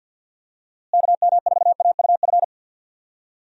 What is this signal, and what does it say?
Morse code, heard as a keyed tone, about 46 wpm, 700 Hz. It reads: KM4AUF
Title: USA Call Signs - 6 Characters - Repeated 3x - 50wpm